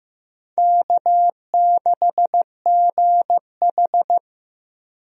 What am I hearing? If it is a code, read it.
Morse code, heard as a keyed tone, about 15 wpm, 700 Hz. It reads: K6GH
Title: USA Call Signs - 4 Characters - 15wpm